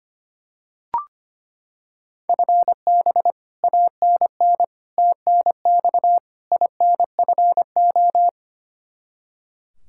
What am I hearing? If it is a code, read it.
Morse code, heard as a keyed tone, about 25 wpm, 700 Hz. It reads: FB ANN TNX INFO
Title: Practice QSO - Exchange 3 - More Info - 25wpm